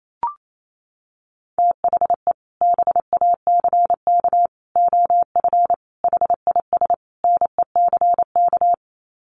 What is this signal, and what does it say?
Morse code, heard as a keyed tone, about 28 wpm, 700 Hz. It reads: THE BACK OF HIS NECK
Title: Sets of 5 Words Encore 28wpm